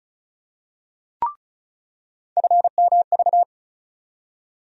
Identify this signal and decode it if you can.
Morse code, heard as a keyed tone, about 36 wpm, 700 Hz. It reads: FMV
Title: USA Call Sign Postfixes - 3 Letters 35wpm